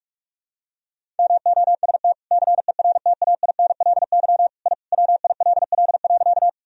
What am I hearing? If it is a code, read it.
Morse code, heard as a keyed tone, about 44 wpm, 700 Hz. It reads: MOST CERTAINLY I WILL.
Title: Sentences from Top 1000 Words - Review Part 1 - 45wpm